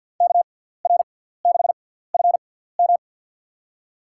Morse code, heard as a keyed tone, about 53 wpm, spaced as for 17 wpm, 700 Hz. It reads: KR6FM
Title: USA Call Signs - 5 Characters - 6x Character Spacing - 50wpm